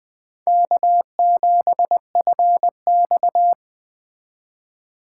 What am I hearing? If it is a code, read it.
Morse code, heard as a keyed tone, about 21 wpm, 700 Hz. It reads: K7FX
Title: Speed Racing - USA 1x2 Call Signs 17wpm